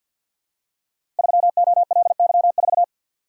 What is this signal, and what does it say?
Morse code, heard as a keyed tone, about 52 wpm, 690 Hz. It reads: 3OLY4
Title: Code Groups - 5 Letter:Numbers - 50wpm